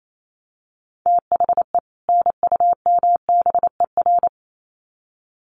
Morse code, heard as a keyed tone, about 28 wpm, 700 Hz. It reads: THE NUMBER
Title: Sets of 2 Words 28wpm